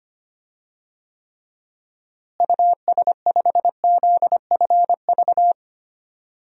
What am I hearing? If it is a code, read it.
Morse code, heard as a keyed tone, about 25 wpm, 700 Hz. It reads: US5ZFV
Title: DXCC-100 Call Signs - 6 Characters - Repeated 3x - 25wpm